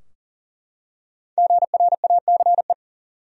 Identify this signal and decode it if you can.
Morse code, heard as a keyed tone, about 40 wpm, 700 Hz. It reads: GRACE
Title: USA Common Names - 40wpm